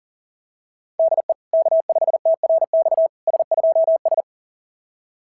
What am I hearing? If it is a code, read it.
Morse code, heard as a keyed tone, about 37 wpm, 660 Hz. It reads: DE K5TRX S2S
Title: SOTA QSO Practice 40wpm